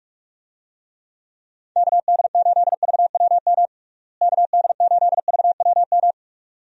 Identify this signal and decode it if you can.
Morse code, heard as a keyed tone, about 46 wpm, 700 Hz. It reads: KD8VWM KD8VWM
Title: USA Call Signs - 6 Characters - Repeated 2x - 45wpm